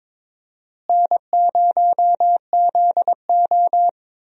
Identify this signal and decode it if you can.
Morse code, heard as a keyed tone, about 22 wpm, 700 Hz. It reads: N0ZO
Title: USA Call Signs - 4 Characters - Repeated 3x - 22wpm